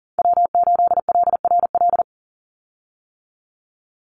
Morse code, heard as a keyed tone, about 40 wpm, 700 Hz. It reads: W8LRL
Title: USA Call Signs 40wpm